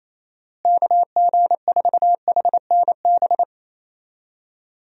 Morse code, heard as a keyed tone, about 28 wpm, 700 Hz. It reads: KG4HNB
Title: USA Call Signs - 6 Characters - Repeated 3x - 28wpm